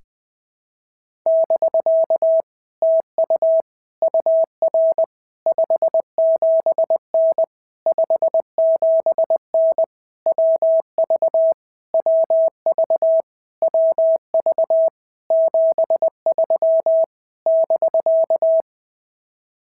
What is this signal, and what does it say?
Morse code, heard as a keyed tone, about 20 wpm, 650 Hz. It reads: <BK> TU UR 57N 57N WV WV WV 73 <BK>
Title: POTA QSO Practice - 20wpm